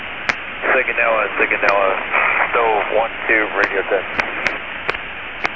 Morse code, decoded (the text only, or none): none